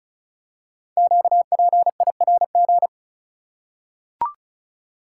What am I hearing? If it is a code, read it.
Morse code, heard as a keyed tone, about 36 wpm, 700 Hz. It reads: QPIRG
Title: Code Groups - 5 Letters - 35wpm